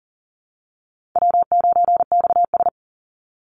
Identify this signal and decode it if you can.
Morse code, heard as a keyed tone, about 40 wpm, 700 Hz. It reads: W9XS
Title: Easy Call Signs Repeated 2x - 40wpm